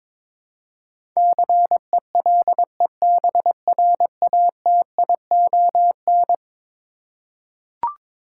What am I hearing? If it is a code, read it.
Morse code, heard as a keyed tone, about 22 wpm, 700 Hz. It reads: CELEBRATION
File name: practice / 22wpm / Eleven Letter Words - 22wpm